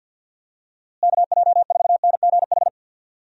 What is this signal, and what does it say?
Morse code, heard as a keyed tone, about 48 wpm, 700 Hz. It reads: KJ4NGH
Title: USA Call Signs - 6 Characters - Repeated 2x - 50wpm